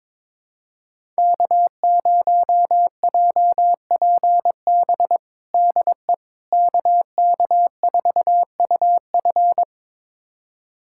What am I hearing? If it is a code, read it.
Morse code, heard as a keyed tone, about 22 wpm, 700 Hz. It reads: K0JPB DE KK4UF
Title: USA Callsign DE Callsign - 5 Characters - 22wpm